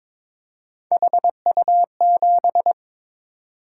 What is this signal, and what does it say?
Morse code, heard as a keyed tone, about 22 wpm, 700 Hz. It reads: HU7
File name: Three Letter-Number Combinations - 22wpm